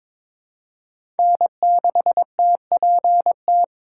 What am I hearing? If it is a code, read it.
Morse code, heard as a keyed tone, about 23 wpm, 700 Hz. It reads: N6TPT
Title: USA Call Signs - 5 Characters - 4x Character Spacing - 22wpm